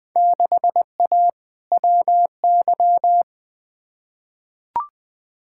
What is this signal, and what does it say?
Morse code, heard as a keyed tone, about 20 wpm, 700 Hz. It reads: 6A WY
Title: Field Day Exchanges Encore - 20wpm